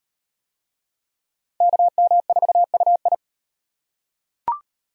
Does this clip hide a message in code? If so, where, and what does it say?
Yes: Morse code from 1.6 to 3.1 s, heard as a keyed tone, about 35 wpm, 700 Hz. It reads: KM4UI